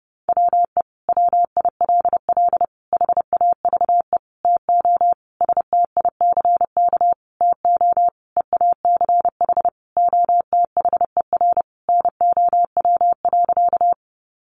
Morse code, heard as a keyed tone, about 30 wpm, 700 Hz. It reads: WE WILL HAVE TO STICK TO EACH OTHER NOW.
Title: Sentences from Top 1200 Words 30wpm